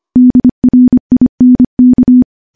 Morse code, heard as a keyed tone, about 25 wpm, 250 Hz. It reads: DRINK
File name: morse-training-25WPM